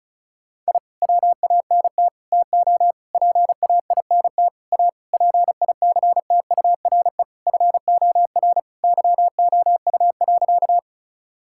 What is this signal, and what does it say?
Morse code, heard as a keyed tone, about 34 wpm, 700 Hz. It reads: I WANT TO PAINT A PICTURE FOR YOU.
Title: Sentences from Top 1000 Words - Review Part 6 - 35wpm